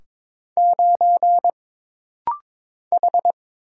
Morse code, heard as a keyed tone, about 22 wpm, 700 Hz. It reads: 9 H
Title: Single Letter-Number - Rapid-Fire 22wpm